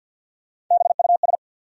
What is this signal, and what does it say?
Morse code, heard as a keyed tone, about 52 wpm, 700 Hz. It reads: BUS
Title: Sentences from Top 1000 Words - Review Part 6 - New Word Drill - 50wpm